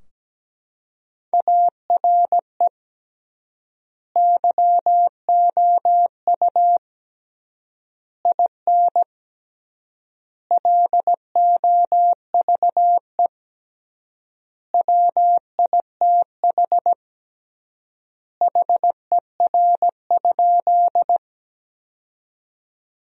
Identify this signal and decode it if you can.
Morse code, heard as a keyed tone, about 17 wpm, 700 Hz. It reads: ARE YOU IN LOVE WITH HER?
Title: 3x Word-Spacing - Sentences from Top 400 Words - 17wpm